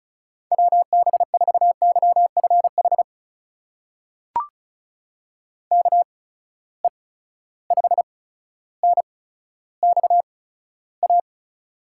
Morse code, heard as a keyed tone, about 36 wpm, spaced as for 15 wpm, 700 Hz. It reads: WB4YFH KE5NXA
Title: USA Call Signs - 6 Characters - 8x Character Spacing - 35wpm